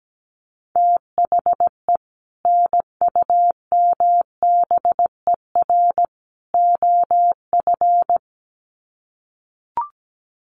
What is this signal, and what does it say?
Morse code, heard as a keyed tone, about 17 wpm, 700 Hz. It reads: THE NUMBER OF
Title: Sets of 3 Words 17wpm